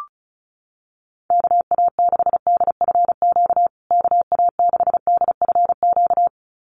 Morse code, heard as a keyed tone, about 34 wpm, 700 Hz. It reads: KA6DFQ KA6DFQ
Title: USA Call Signs Repeated 2x 35wpm